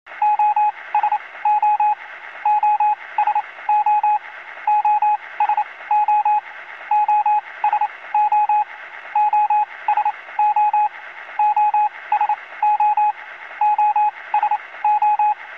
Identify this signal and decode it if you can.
Morse code, heard as a keyed tone, about 28 wpm, spaced as for 20 wpm, 840 Hz, with radio noise. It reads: OSO OSO OSO OSO OSO OSO OSO